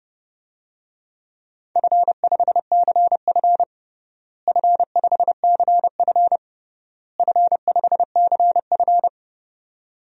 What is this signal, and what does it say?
Morse code, heard as a keyed tone, about 32 wpm, 700 Hz. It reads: F5CF F5CF F5CF
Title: DXCC-100 Call Signs - 4 Characters - Repeated 3x - 30wpm